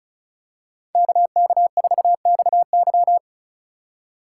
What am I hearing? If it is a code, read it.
Morse code, heard as a keyed tone, about 36 wpm, 700 Hz. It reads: KK4XY
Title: USA Call Signs - 5 Characters - 2x Character Spacing - 35wpm